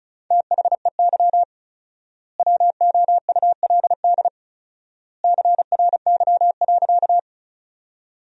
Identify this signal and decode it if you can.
Morse code, heard as a keyed tone, about 34 wpm, 700 Hz. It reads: THEY WOULD CRY.
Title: 4x Word-Spacing - Sentences from Top 500 Words - 35wpm